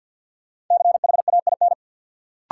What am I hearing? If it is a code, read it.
Morse code, heard as a keyed tone, about 51 wpm, 700 Hz. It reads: CHAIN
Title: Sentences from Top 1500 Words - New Word Drill Alt - 50wpm